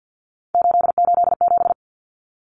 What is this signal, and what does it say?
Morse code, heard as a keyed tone, about 48 wpm, 700 Hz. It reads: 887
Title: Common Cut Numbers - 3-Digit 50wpm